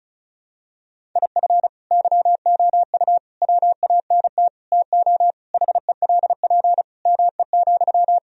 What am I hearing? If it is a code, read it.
Morse code, heard as a keyed tone, about 36 wpm, 700 Hz. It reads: IF YOU WANT TO HELP ME,
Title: Sentences from Top 1000 Words - Review Part 1 - 35wpm